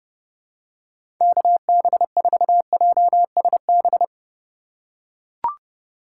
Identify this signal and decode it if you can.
Morse code, heard as a keyed tone, about 30 wpm, 700 Hz. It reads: KB4JSB
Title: USA Call Signs - 6 Characters - 3x Character Spacing - 30wpm